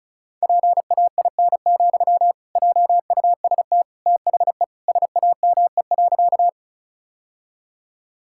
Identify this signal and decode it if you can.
Morse code, heard as a keyed tone, about 34 wpm, 700 Hz. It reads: PAIN, JUST THE SAME.
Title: Sentences from Top 1600 Words 35wpm